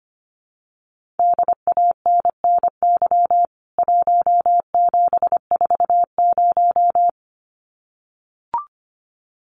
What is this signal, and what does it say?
Morse code, heard as a keyed tone, about 25 wpm, 700 Hz. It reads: DANNY 1740
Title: CWT Exchange 25wpm